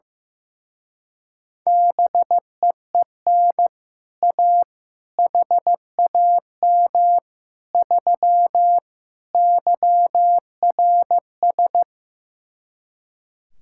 Morse code, heard as a keyed tone, about 15 wpm, 700 Hz. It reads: BEEN A HAM 3 YRS
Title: Speed-Racing Common QSO Phrases Encore - 15wpm